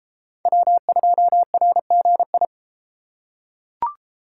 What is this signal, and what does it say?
Morse code, heard as a keyed tone, about 32 wpm, 700 Hz. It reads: W2RGI